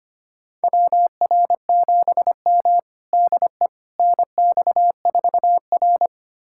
Morse code, heard as a keyed tone, about 25 wpm, 700 Hz. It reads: WR7M DE NX4R
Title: USA Callsign DE Callsign - 4 Characters - 25wpm